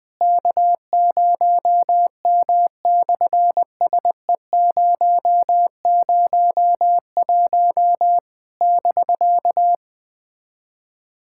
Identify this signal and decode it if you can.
Morse code, heard as a keyed tone, about 20 wpm, 700 Hz. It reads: K0M/SE001 <BK>
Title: SOTA QSO Practice 20wpm